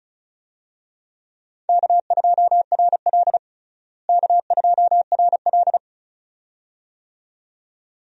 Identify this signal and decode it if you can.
Morse code, heard as a keyed tone, about 34 wpm, 700 Hz. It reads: K2RL K2RL
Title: USA Call Signs - 4 Characters - Repeated 2x - 35wpm